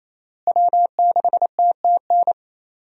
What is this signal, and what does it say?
Morse code, heard as a keyed tone, about 28 wpm, 700 Hz. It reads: W6TTN
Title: USA Call Signs - 5 Characters - Repeated 2x - 28wpm